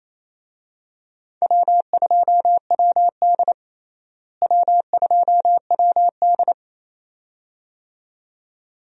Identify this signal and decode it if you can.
Morse code, heard as a keyed tone, about 28 wpm, 700 Hz. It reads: W2WD W2WD
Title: USA Call Signs - 4 Characters - Repeated 2x - 28wpm